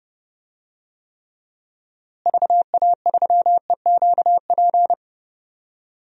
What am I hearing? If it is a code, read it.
Morse code, heard as a keyed tone, about 30 wpm, 700 Hz. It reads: VA3EQP
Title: DXCC-100 Call Signs - 6 Characters - Repeated 3x - 30wpm